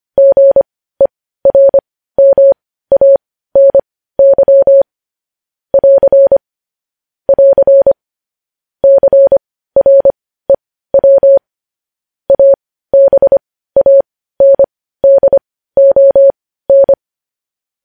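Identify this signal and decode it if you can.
Morse code, heard as a keyed tone, about 25 wpm, spaced as for 15 wpm, 550 Hz. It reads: GERMANY <AR> <AR> CREW ABANDON
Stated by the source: Morse code news headlines.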